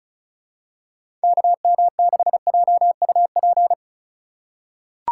Morse code, heard as a keyed tone, about 36 wpm, 700 Hz. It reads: KM6JUP
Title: USA Call Signs - 6 Characters - 2x Character Spacing - 35wpm